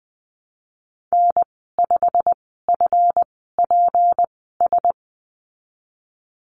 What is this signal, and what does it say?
Morse code, heard as a keyed tone, about 19 wpm, spaced as for 14 wpm, 700 Hz. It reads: N5FPS
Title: 200 Lesson 32g - Letter k - Call Signs 20wpm